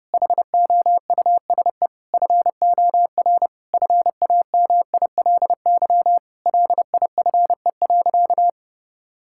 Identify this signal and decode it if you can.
Morse code, heard as a keyed tone, about 32 wpm, 700 Hz. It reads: HOUSE FOR FAMILY LIFE.
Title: Sentences from Top 1000 Words - Review Part 5 - 30wpm